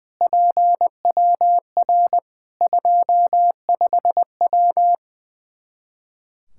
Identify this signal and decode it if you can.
Morse code, heard as a keyed tone, about 20 wpm, 700 Hz. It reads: PWR 25W
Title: K1USN SST QSO Practice 20wpm